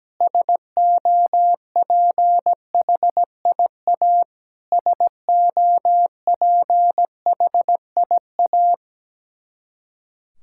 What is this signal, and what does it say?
Morse code, heard as a keyed tone, about 17 wpm, 700 Hz. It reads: SOPHIA SOPHIA
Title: Practice QSO - Exchange 1 - Beginning - 17wpm